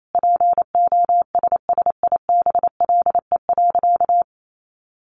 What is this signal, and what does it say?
Morse code, heard as a keyed tone, about 28 wpm, 700 Hz. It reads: POSSIBLE.
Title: Sentences from Top 1400 Words 28wpm